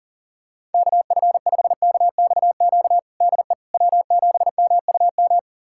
Morse code, heard as a keyed tone, about 40 wpm, 700 Hz. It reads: KF5KXQ DE W7MUM
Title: USA Callsign DE Callsign 40wpm